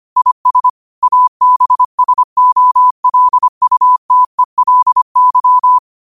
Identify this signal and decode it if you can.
Morse code, heard as a keyed tone, about 25 wpm, 1000 Hz. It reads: IS ABSOLUTELY
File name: qotdfiles / 25wpm